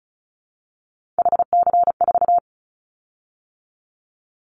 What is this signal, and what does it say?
Morse code, heard as a keyed tone, about 36 wpm, 700 Hz. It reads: HC4
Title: Callsign Prefixes for Top 100 DXCC - 35wpm